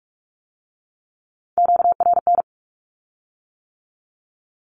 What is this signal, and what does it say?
Morse code, heard as a keyed tone, about 46 wpm, 700 Hz. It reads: QRN
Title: Q-Codes with Spoken Meaning - 45wpm